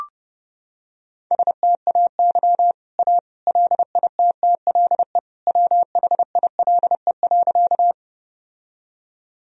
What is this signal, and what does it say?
Morse code, heard as a keyed tone, about 30 wpm, 700 Hz. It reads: STAY A LITTLE WHILE.
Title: Sentences from Top 1000 Words - Review Part 1 - 30wpm